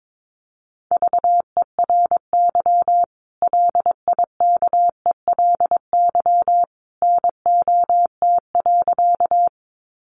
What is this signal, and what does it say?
Morse code, heard as a keyed tone, about 22 wpm, 700 Hz. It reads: VERY LIKELY NOT.
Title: Sentences from Top 600 Words 22wpm